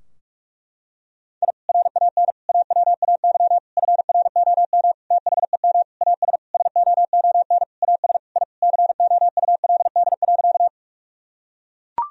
Morse code, heard as a keyed tone, about 41 wpm, 700 Hz. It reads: I RAN AWAY FROM THEM AS SOON AS I COULD.
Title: Sentences from Top 1000 Words - Review Part 3 - 45wpm